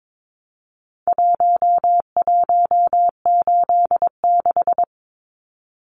Morse code, heard as a keyed tone, about 22 wpm, 700 Hz. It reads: 1186
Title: Numbers - Contest Style - 4-Digit 22wpm